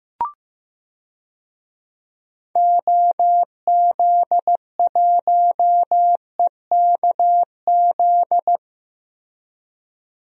Morse code, heard as a keyed tone, about 15 wpm, 700 Hz. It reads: OZ1EKZ
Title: DXCC-100 Call Signs - 6 Characters - Repeated 3x - 15wpm